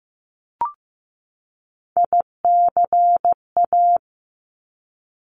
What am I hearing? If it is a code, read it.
Morse code, heard as a keyed tone, about 15 wpm, 700 Hz. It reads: ICA
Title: Top Three Letter Combinations in Words 15wpm